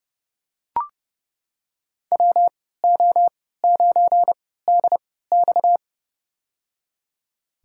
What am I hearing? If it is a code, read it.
Morse code, heard as a keyed tone, about 30 wpm, spaced as for 17 wpm, 700 Hz. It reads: WO9DX
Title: USA Call Signs - 3x Character Spacing - 30wpm